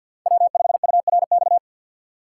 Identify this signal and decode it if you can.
Morse code, heard as a keyed tone, about 51 wpm, 700 Hz. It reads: W5URX
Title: USA Call Signs - 5 Characters - 6x Character Spacing - 50wpm